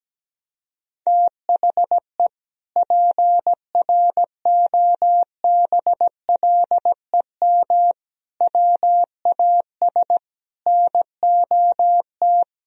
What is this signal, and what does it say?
Morse code, heard as a keyed tone, about 17 wpm, 700 Hz. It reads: THE PROBLEM WAS NOT
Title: Sentences from Top 1000 Words - Review Part 2 - 17wpm